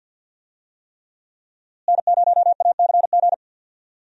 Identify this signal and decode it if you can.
Morse code, heard as a keyed tone, about 52 wpm, 700 Hz. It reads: N0ACG
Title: USA Call Signs - 5 Characters - Repeated 3x - 50wpm